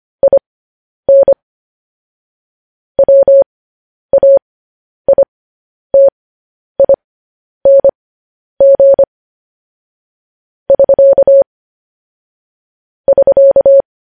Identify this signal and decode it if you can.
Morse code, heard as a keyed tone, about 25 wpm, spaced as for 10 wpm, 550 Hz. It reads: IN WAITING <SK> <SK>
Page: news-headlines-in-morse-code-at-10-wpm